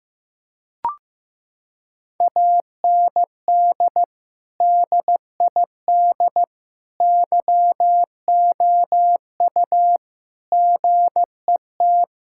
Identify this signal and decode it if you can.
Morse code, heard as a keyed tone, about 16 wpm, 700 Hz. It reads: AND DID YOU GET
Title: Sentences from Top 1000 Words - Review Part 6 - 15wpm